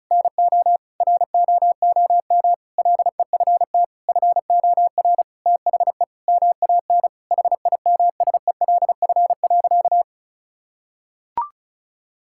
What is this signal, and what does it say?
Morse code, heard as a keyed tone, about 36 wpm, 700 Hz. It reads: NO ROOM LEFT FOR THE MAN HIMSELF.
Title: Sentences from Top 1600 Words 35wpm